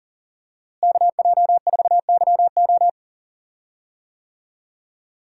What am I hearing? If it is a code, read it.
Morse code, heard as a keyed tone, about 40 wpm, 700 Hz. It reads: KJ4YO
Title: USA Call Signs - 5 Characters - Repeated 2x - 40wpm